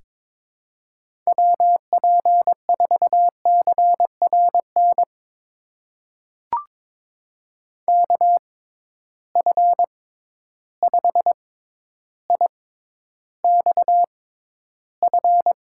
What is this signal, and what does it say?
Morse code, heard as a keyed tone, about 22 wpm, spaced as for 11 wpm, 700 Hz. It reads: WP4CRN KF5IXF
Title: USA Call Signs - 6 Characters - 6x Character Spacing - 22wpm